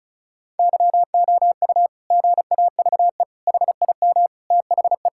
Morse code, heard as a keyed tone, about 36 wpm, 690 Hz. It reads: YOU GAVE HIM THE
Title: Sentences from Top 1000 Words - Review Part 6 - 35wpm